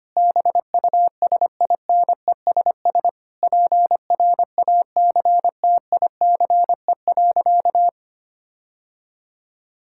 Morse code, heard as a keyed tone, about 25 wpm, 700 Hz. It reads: BUSINESS PRACTICE.
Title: Sentences from Top 1000 Words - Review Part 6 - 25wpm